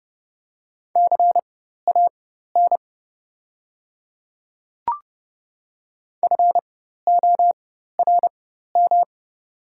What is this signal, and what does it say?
Morse code, heard as a keyed tone, about 32 wpm, spaced as for 14 wpm, 700 Hz. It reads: CAN FORM
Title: 4x Character Spacing - Top 300 Words - 30wpm